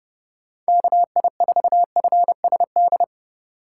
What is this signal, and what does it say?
Morse code, heard as a keyed tone, about 30 wpm, 700 Hz. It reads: KI4FSD
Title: USA Call Signs - 6 Characters - 2x Character Spacing - 30wpm